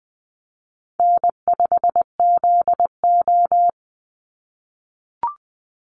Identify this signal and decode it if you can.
Morse code, heard as a keyed tone, about 20 wpm, 700 Hz. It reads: N5ZO
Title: Call Signs - 20wpm